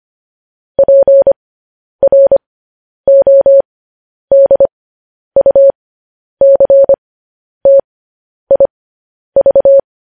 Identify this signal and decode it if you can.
Morse code, heard as a keyed tone, about 25 wpm, spaced as for 10 wpm, 550 Hz. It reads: PRODUCTIV
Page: news-headlines-in-morse-code-at-10-wpm